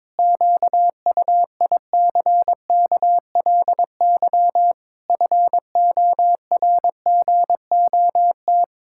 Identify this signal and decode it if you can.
Morse code, heard as a keyed tone, about 22 wpm, 700 Hz. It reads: QUICKLY FORGOT
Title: Pangram Challenge v2 - 22wpm